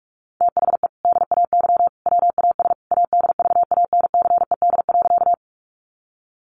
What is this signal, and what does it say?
Morse code, heard as a keyed tone, about 44 wpm, 700 Hz. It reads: THE DAY WAS ADVANCED.
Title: Sentences from Top 1200 Words 45wpm